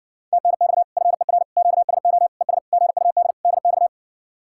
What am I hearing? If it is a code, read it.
Morse code, heard as a keyed tone, about 60 wpm, 700 Hz. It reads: TNX FER QSO ES GUD DX